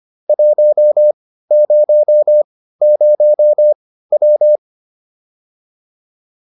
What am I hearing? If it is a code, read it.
Morse code, heard as a keyed tone, about 25 wpm, spaced as for 15 wpm, 600 Hz. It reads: 100W